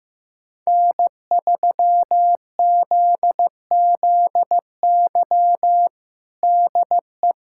Morse code, heard as a keyed tone, about 15 wpm, 700 Hz. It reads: N3ZZY DE
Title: USA Callsign DE Callsign - 5 Characters - 15wpm